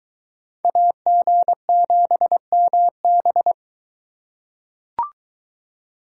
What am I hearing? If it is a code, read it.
Morse code, heard as a keyed tone, about 21 wpm, 700 Hz. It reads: AG7MB